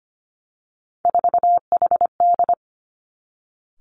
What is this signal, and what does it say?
Morse code, heard as a keyed tone, about 25 wpm, 700 Hz. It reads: 4HD
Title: Three Challenging Letter-Number Combinations 25wpm